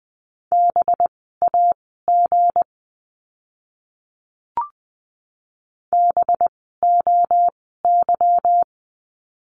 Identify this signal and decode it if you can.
Morse code, heard as a keyed tone, about 20 wpm, spaced as for 15 wpm, 700 Hz. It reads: BAG BOY
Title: 185 Lesson 30f - Letter b - Words 20wpm